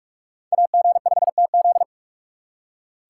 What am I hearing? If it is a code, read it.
Morse code, heard as a keyed tone, about 39 wpm, 700 Hz. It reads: AG5TZ